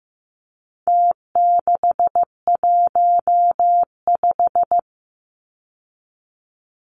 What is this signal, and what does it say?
Morse code, heard as a keyed tone, about 15 wpm, 700 Hz. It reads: T615
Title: Numbers - Contest Style - 4-Digit 15wpm